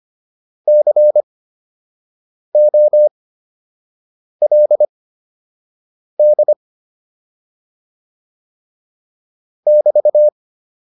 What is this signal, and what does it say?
Morse code, heard as a keyed tone, about 25 wpm, spaced as for 6 wpm, 600 Hz. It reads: COLD <BT>